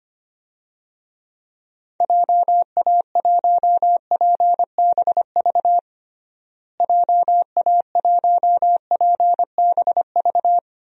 Morse code, heard as a keyed tone, about 25 wpm, 700 Hz. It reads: JA1PBV JA1PBV
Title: DXCC-100 Call Signs - 6 Characters - Repeated 2x - 25wpm